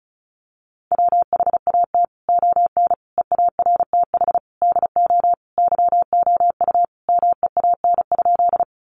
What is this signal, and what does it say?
Morse code, heard as a keyed tone, about 34 wpm, 700 Hz. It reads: WHAT ON EARTH DO YOU MEAN?
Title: Sentences from Top 200 Words 35wpm